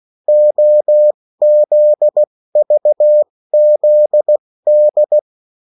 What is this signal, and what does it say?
Morse code, heard as a keyed tone, about 16 wpm, spaced as for 14 wpm, 600 Hz. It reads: OZVZD
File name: morse mp3 / revision9-vitesse 14 mots